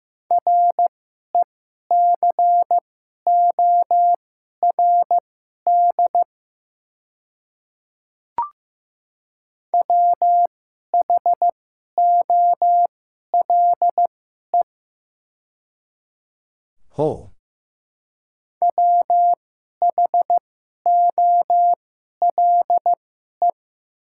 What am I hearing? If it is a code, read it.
Morse code, heard as a keyed tone, about 15 wpm, spaced as for 11 wpm, 700 Hz. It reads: RECORD WHOLE WHOLE
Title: 2x Character Spacing - Top 500 Words - 15wpm